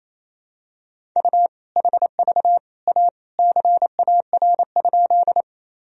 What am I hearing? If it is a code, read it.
Morse code, heard as a keyed tone, about 28 wpm, 700 Hz. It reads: U HV A CAR?
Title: Sentences from Top 300 Words and Abbreviations 28wpm